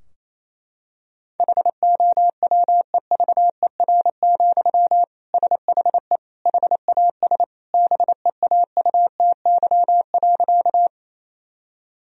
Morse code, heard as a keyed tone, about 27 wpm, 700 Hz. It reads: HOWEVER, SHE HAS BEAUTY.
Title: Sentences from Top 1000 Words - Review Part 2 - 28wpm